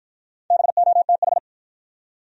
Morse code, heard as a keyed tone, about 53 wpm, 700 Hz. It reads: BOTH